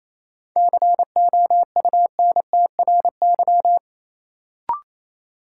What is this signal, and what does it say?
Morse code, heard as a keyed tone, about 28 wpm, 700 Hz. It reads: COUNTRY
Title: Sentences from Top 300 Words - New Word Drill - 28wpm